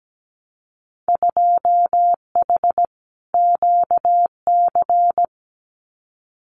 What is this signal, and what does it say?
Morse code, heard as a keyed tone, about 17 wpm, 700 Hz. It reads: 2H QC
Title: Winter Field Day Exchanges Encore 17wpm